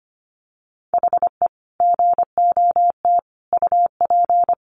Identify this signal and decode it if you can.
Morse code, heard as a keyed tone, about 25 wpm, 700 Hz. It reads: HE GOT UP
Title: Sentences from Top 1200 Words 25wpm